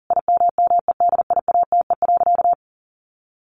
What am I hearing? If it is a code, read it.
Morse code, heard as a keyed tone, about 40 wpm, 700 Hz. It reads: IMMEDIATE.
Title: Sentences from Top 1200 Words 40wpm